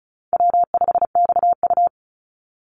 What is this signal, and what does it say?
Morse code, heard as a keyed tone, about 36 wpm, 700 Hz. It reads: W5XU
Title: Call Signs - 35wpm